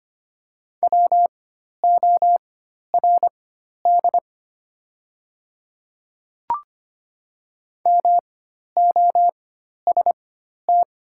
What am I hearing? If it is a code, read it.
Morse code, heard as a keyed tone, about 25 wpm, spaced as for 12 wpm, 700 Hz. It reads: WORD MOST
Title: 4x Character Spacing - Top 100 Words - 25wpm